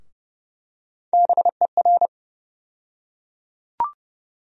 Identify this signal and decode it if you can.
Morse code, heard as a keyed tone, about 30 wpm, 700 Hz. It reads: BER
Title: USA Call Sign Postfixes - 3 Letters 30wpm